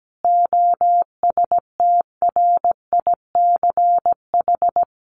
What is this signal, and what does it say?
Morse code, heard as a keyed tone, about 17 wpm, 700 Hz. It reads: OSTRICH
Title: A Sound Of Thunder 17wpm